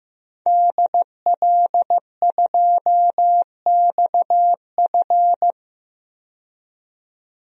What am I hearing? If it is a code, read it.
Morse code, heard as a keyed tone, about 15 wpm, 700 Hz. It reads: DL2XF
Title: DXCC-100 Call Signs - 5 Characters - 15wpm